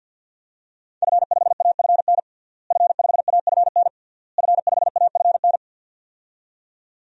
Morse code, heard as a keyed tone, about 47 wpm, 700 Hz. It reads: F5AFN F5AFN F5AFN
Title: DXCC-100 Call Signs - 5 Characters - Repeated 3x - 50wpm